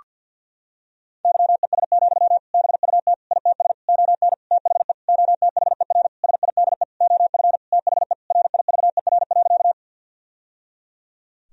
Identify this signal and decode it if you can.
Morse code, heard as a keyed tone, about 52 wpm, 700 Hz. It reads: YES, BUT ITS ON THE OTHER SIDE OF THE RIVER.
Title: Sentences from Top 1000 Words - Review Part 1 - 50wpm